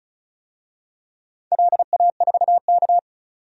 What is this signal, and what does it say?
Morse code, heard as a keyed tone, about 34 wpm, 700 Hz. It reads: LA4K
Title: DXCC-100 Call Signs - 4 Characters - 35wpm